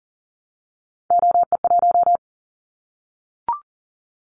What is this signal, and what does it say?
Morse code, heard as a keyed tone, about 40 wpm, 700 Hz. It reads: OE1
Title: Callsign Prefixes for Top 25 DXCC - Repeated 3x - 40wpm